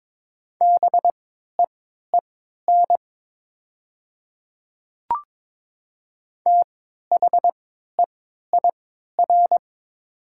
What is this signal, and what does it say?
Morse code, heard as a keyed tone, about 22 wpm, spaced as for 13 wpm, 700 Hz. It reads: BEEN THEIR
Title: 3x Character Spacing - Top 200 Words - 22wpm